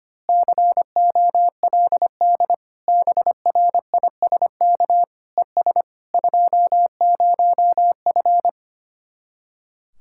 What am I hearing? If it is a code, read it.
Morse code, heard as a keyed tone, about 24 wpm, 700 Hz. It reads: COLD BRISK ES 20F
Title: Speed-Racing Common QSO Phrases - 25wpm